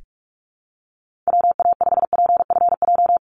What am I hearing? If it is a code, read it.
Morse code, heard as a keyed tone, about 44 wpm, 700 Hz. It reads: WA5PFJ
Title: Call Signs Repeated 2x - 45wpm